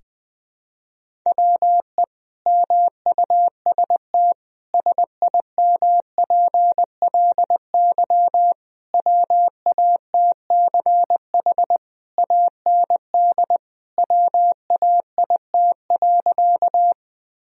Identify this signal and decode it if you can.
Morse code, heard as a keyed tone, about 21 wpm, 700 Hz. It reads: WE MUST SIMPLY WATCH AND WAIT.
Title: Sentences from Top 1000 Words - Review Part 4 - 20wpm